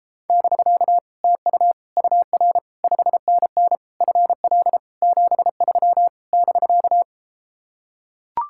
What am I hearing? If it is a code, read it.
Morse code, heard as a keyed tone, about 33 wpm, 700 Hz. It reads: <BK> TU UR 5NN FL 73 <BK>